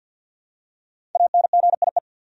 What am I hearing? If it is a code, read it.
Morse code, heard as a keyed tone, about 47 wpm, 700 Hz. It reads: ANGIE
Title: USA Common Names - 50wpm